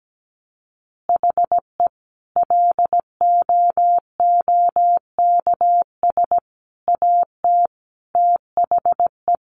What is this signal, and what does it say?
Morse code, heard as a keyed tone, about 17 wpm, 700 Hz. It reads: HE LOOKS AT THE
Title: Additional Easy Phrases 17wpm